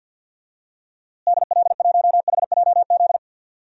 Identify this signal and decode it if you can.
Morse code, heard as a keyed tone, about 50 wpm, 700 Hz. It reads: DL1HJZ
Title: DXCC-100 Call Signs - 6 Characters - Repeated 3x - 50wpm